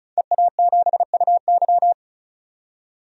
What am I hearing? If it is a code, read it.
Morse code, heard as a keyed tone, about 36 wpm, 700 Hz. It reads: EA7UY
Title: DXCC-100 Call Signs - 5 Characters - 35wpm